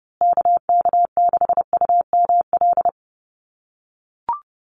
Morse code, heard as a keyed tone, about 30 wpm, 700 Hz. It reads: KK6UML
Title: USA Call Signs Repeated 3x 30wpm